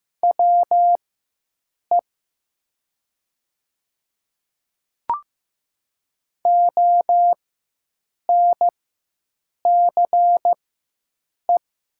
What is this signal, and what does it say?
Morse code, heard as a keyed tone, about 15 wpm, spaced as for 7 wpm, 700 Hz. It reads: WE ONCE
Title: 4x Character Spacing - Top 300 Words - 15wpm